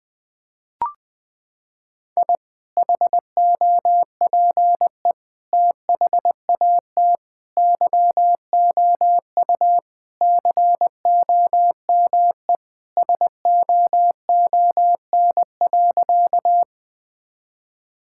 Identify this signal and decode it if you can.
Morse code, heard as a keyed tone, about 21 wpm, 700 Hz. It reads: I HOPE THAT YOU COME SOON.
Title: Sentences with 2-Word Phrases - 20wpm